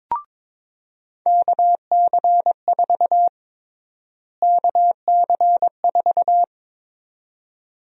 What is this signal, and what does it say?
Morse code, heard as a keyed tone, about 22 wpm, 700 Hz. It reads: KC4 KC4
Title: USA Call Sign Prefixes - Repeated 2x - 22wpm